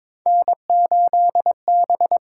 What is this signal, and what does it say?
Morse code, heard as a keyed tone, about 22 wpm, 700 Hz. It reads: N8B
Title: USA Call Signs - 5 Characters - 6x Character Spacing - 22wpm